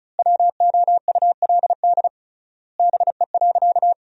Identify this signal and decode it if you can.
Morse code, heard as a keyed tone, about 34 wpm, 700 Hz. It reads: WOULD BE.
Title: 3x Word-Spacing - Sentences from Top 200 Words - 35wpm